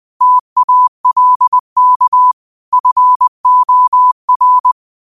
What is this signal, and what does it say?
Morse code, heard as a keyed tone, about 20 wpm, 1000 Hz. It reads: TALK FOR
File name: qotdfiles / 20wpm